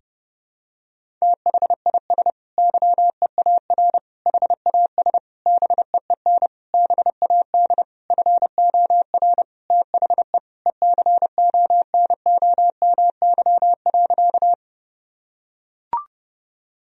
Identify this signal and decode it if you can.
Morse code, heard as a keyed tone, about 32 wpm, 700 Hz. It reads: THIS YEAR HAS BEEN BAD FOR THE ECONOMY.
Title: Sentences from Top 1000 Words - Review Part 3 - 30wpm